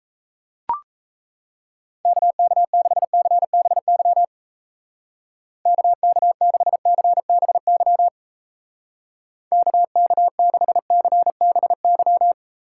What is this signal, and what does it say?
Morse code, heard as a keyed tone, about 34 wpm, 700 Hz. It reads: KK6CBY KK6CBY KK6CBY
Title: Speed Racing - USA 2x3 Call Signs 28wpm